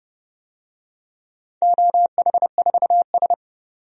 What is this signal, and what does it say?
Morse code, heard as a keyed tone, about 30 wpm, 700 Hz. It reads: OH4S
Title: DXCC-100 Call Signs - 4 Characters - 30wpm